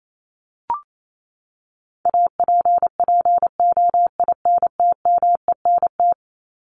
Morse code, heard as a keyed tone, about 28 wpm, 700 Hz. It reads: APPOINTMENT
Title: Top 1751-2000 Words 28wpm